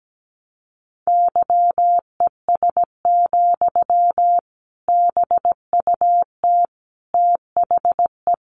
Morse code, heard as a keyed tone, about 17 wpm, 700 Hz. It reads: YES, BUT THE
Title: Sentences from Top 1200 Words 17wpm